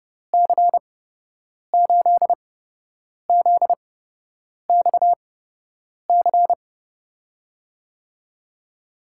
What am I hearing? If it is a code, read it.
Morse code, heard as a keyed tone, about 32 wpm, spaced as for 8 wpm, 700 Hz. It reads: C8ZXC
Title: USA Call Signs - 6 Characters - 8x Character Spacing - 30wpm